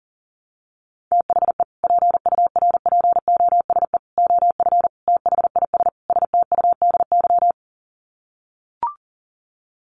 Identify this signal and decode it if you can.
Morse code, heard as a keyed tone, about 40 wpm, 700 Hz. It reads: THE PURPOSE OF THIS STUDY
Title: Sets of 5 Words 40wpm